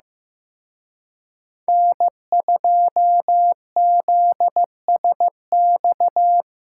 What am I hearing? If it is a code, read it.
Morse code, heard as a keyed tone, about 18 wpm, 700 Hz. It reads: N2ZSX
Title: Speed Racing - USA 1x3 Call Signs 15wpm